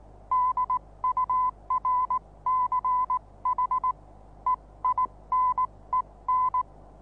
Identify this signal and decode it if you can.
Morse code, heard as a keyed tone, about 19 wpm, spaced as for 16 wpm, 1010 Hz. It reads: DURCH EINEN
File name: repos / Morsekurs / DJ4UF Morsen26